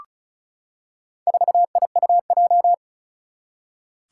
Morse code, heard as a keyed tone, about 36 wpm, 690 Hz. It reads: 4IUJ
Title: Four Challenging Letter-Number Combinations 35wpm